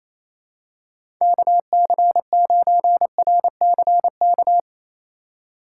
Morse code, heard as a keyed tone, about 28 wpm, 700 Hz. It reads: KC9RCK
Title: USA Call Signs - 6 Characters - 2x Character Spacing - 28wpm